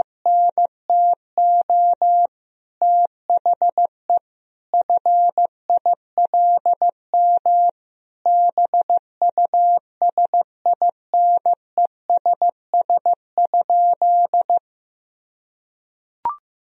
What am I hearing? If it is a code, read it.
Morse code, heard as a keyed tone, about 15 wpm, 700 Hz. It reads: NTO THE FILM BUSINESS?
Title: Sentences from Top 1000 Words - Review Part 3 - 15wpm